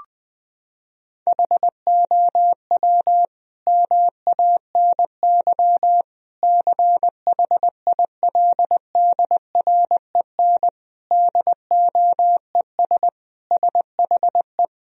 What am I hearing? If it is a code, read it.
Morse code, heard as a keyed tone, about 21 wpm, 700 Hz. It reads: HOW MANY CHILDREN DOES SHE
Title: Sentences from Top 1000 Words - Review Part 1 - 20wpm